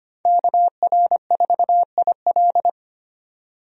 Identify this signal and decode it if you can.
Morse code, heard as a keyed tone, about 25 wpm, 700 Hz. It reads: KR4IL
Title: USA Call Signs - 5 Characters - 25wpm